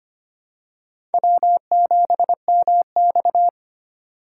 Morse code, heard as a keyed tone, about 25 wpm, 700 Hz. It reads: W7MX
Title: USA Call Signs - 4 Characters - 3x Character Spacing - 25wpm